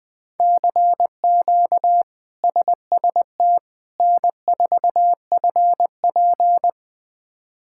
Morse code, heard as a keyed tone, about 20 wpm, 700 Hz. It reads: CQ SST N4FP
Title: K1USN SST QSO Practice - 20wpm